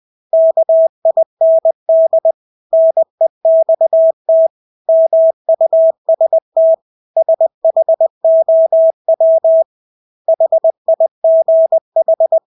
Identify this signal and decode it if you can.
Morse code, heard as a keyed tone, about 20 wpm, 650 Hz. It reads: KIND NEXT MUST SHOW HIGH
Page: Engelske ord 20 wpm